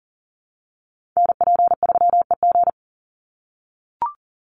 Morse code, heard as a keyed tone, about 40 wpm, 700 Hz. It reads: NP3EG
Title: USA Call Signs 40wpm